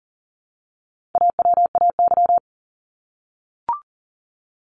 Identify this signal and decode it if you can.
Morse code, heard as a keyed tone, about 40 wpm, 700 Hz. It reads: AWAY
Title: Top 501-750 Words 40wpm